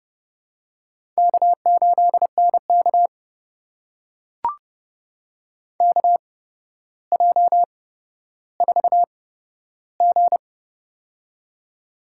Morse code, heard as a keyed tone, about 31 wpm, spaced as for 13 wpm, 700 Hz. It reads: K8NK KJ4G
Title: USA Call Signs - 4 Characters - 8x Character Spacing - 30wpm